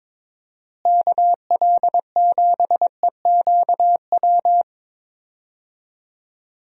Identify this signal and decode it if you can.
Morse code, heard as a keyed tone, about 22 wpm, 700 Hz. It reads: KL7EQW
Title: DXCC-100 Call Signs - 6 Characters - Repeated 3x - 22wpm